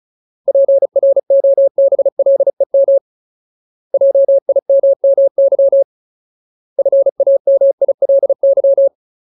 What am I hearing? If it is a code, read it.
Morse code, heard as a keyed tone, about 34 wpm, 550 Hz. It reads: PROBLEM JIMMY FAMILY
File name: Ultimate Word List / 35WPM Ultimate